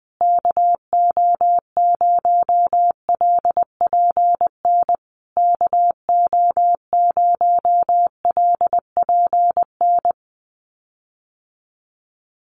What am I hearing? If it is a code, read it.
Morse code, heard as a keyed tone, about 19 wpm, 700 Hz. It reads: KO0LPN KO0LPN
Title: USA Call Signs Repeated 3x 20wpm